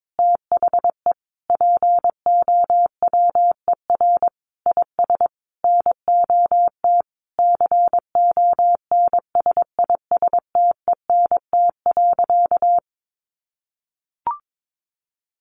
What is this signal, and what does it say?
Morse code, heard as a keyed tone, about 22 wpm, 700 Hz. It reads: THE POWER IS NOT CONSISTENT.
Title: Sentences from Top 1500 Words 22wpm